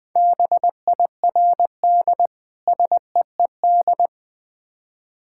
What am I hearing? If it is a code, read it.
Morse code, heard as a keyed tone, about 20 wpm, 690 Hz. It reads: BIRD SEED
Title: Difficult Words with Hints - 20wpm